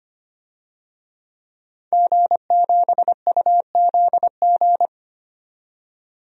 Morse code, heard as a keyed tone, about 25 wpm, 700 Hz. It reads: G7UZG
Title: DXCC-100 Call Signs - 5 Characters - 25wpm